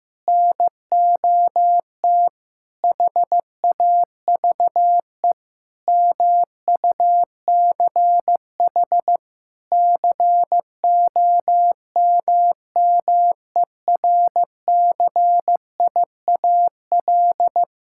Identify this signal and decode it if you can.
Morse code, heard as a keyed tone, about 15 wpm, 700 Hz. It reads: NOT HAVE MUCH COMMERCIAL
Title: Sentences from Top 1000 Words - Review Part 5 - 15wpm